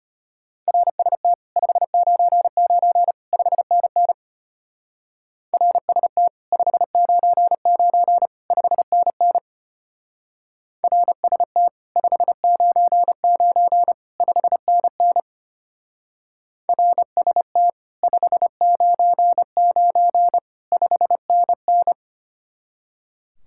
Morse code, heard as a keyed tone, about 30 wpm, 700 Hz. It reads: RST 599 5NN RST 599 5NN RST 599 5NN RST 599 5NN
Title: Speed-Racing Common QSO Phrases - 25wpm